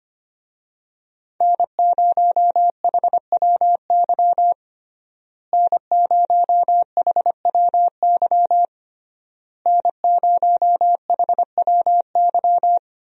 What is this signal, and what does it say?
Morse code, heard as a keyed tone, about 25 wpm, 700 Hz. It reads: N0HWY N0HWY N0HWY
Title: USA Call Signs - 5 Characters - Repeated 3x - 25wpm